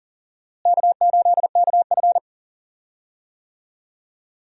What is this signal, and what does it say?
Morse code, heard as a keyed tone, about 40 wpm, 700 Hz. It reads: K8KF
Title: USA Call Signs - 4 Characters - Repeated 2x - 40wpm